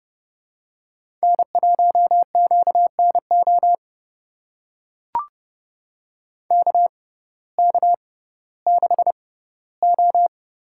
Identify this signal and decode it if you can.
Morse code, heard as a keyed tone, about 30 wpm, spaced as for 17 wpm, 700 Hz. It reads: N1QNO KK6O
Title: USA Call Signs - 6x Character Spacing - 30wpm